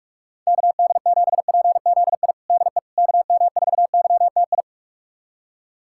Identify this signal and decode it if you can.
Morse code, heard as a keyed tone, about 46 wpm, 700 Hz. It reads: KD7PZI DE KM4YTI
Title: USA Callsign DE Callsign - 6 Characters - 45wpm